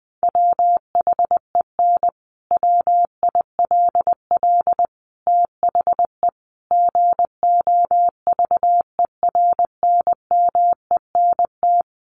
Morse code, heard as a keyed tone, about 20 wpm, 700 Hz. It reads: WHEN WILL THE GOVERNMENT
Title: Sentences from Top 1400 Words 20wpm